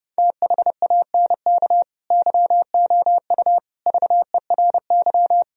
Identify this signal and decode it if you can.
Morse code, heard as a keyed tone, about 33 wpm, 700 Hz. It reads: THANK YOU VERY